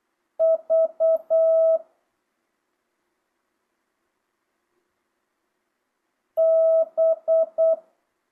Morse code, heard as a keyed tone, about 8 wpm, 650 Hz. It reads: V B
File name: jh8cbh / 2025cw